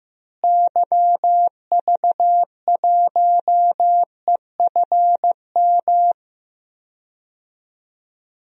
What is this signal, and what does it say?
Morse code, heard as a keyed tone, about 15 wpm, 700 Hz. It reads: YV1EFM
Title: DXCC-100 Call Signs - 6 Characters - Repeated 3x - 15wpm